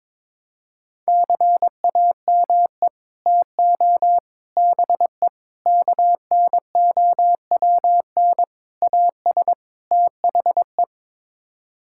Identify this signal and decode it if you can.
Morse code, heard as a keyed tone, about 22 wpm, 700 Hz. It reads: CAME TO BE KNOWN AS THE
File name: practice / 22wpm / Sets of 6 Words - 22wpm